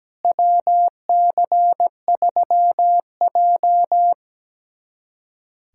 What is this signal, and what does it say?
Morse code, heard as a keyed tone, about 17 wpm, 700 Hz. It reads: WC3J
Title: USA Callsign DE Callsign - 4 Characters - 17wpm